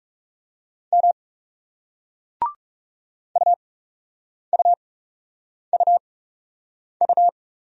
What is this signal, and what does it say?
Morse code, heard as a keyed tone, about 38 wpm, spaced as for 8 wpm, 700 Hz. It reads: M UUUU